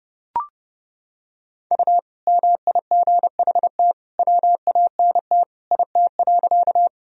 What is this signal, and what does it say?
Morse code, heard as a keyed tone, about 30 wpm, 700 Hz. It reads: U MIGHT WANT IT.
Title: Sentences from Top 300 Words and Abbreviations 30wpm